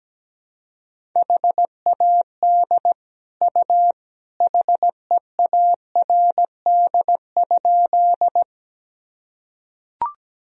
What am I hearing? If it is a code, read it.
Morse code, heard as a keyed tone, about 17 wpm, 700 Hz. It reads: HAD U HEARD?
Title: Sentences from Top 400 Words and Abbreviations 17wpm